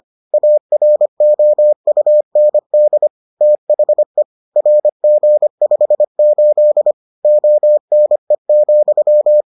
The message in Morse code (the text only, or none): AROUND THE RG58 ONE,